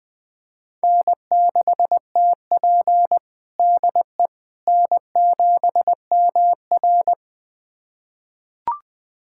Morse code, heard as a keyed tone, about 21 wpm, 700 Hz. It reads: N6TP DE N7MR
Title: USA Callsign DE Callsign - 4 Characters - 20wpm